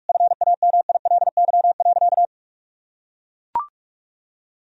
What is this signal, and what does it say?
Morse code, heard as a keyed tone, about 41 wpm, 700 Hz. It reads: FAMILY.
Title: Sentences from Top 1000 Words - Review Part 4 - 45wpm